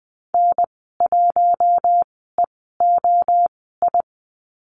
Morse code, heard as a keyed tone, about 20 wpm, spaced as for 14 wpm, 700 Hz. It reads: N1EOI
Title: 042 Lesson 8g - Number 1 - Call Signs 20wpm